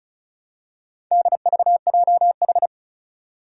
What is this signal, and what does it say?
Morse code, heard as a keyed tone, about 36 wpm, 700 Hz. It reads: DVJH
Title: Four Challenging Letter-Number Combinations 35wpm